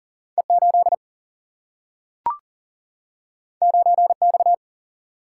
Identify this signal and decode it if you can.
Morse code, heard as a keyed tone, about 40 wpm, 700 Hz. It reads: E8 9X
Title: Single-Letter Single-Number 40wpm